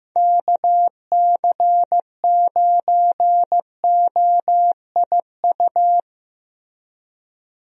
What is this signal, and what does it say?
Morse code, heard as a keyed tone, about 15 wpm, 700 Hz. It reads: KC9OIU
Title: USA Call Signs - 6 Characters - 15wpm